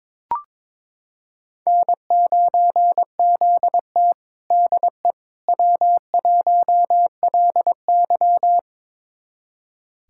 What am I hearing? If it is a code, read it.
Morse code, heard as a keyed tone, about 22 wpm, 700 Hz. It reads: N9ZT DE W1LY
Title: USA Callsign DE Callsign - 4 Characters - 22wpm